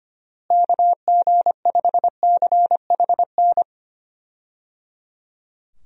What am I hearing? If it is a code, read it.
Morse code, heard as a keyed tone, about 25 wpm, 700 Hz. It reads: KG5CHN
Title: USA Call Signs - 6 Characters - Repeated 3x - 25wpm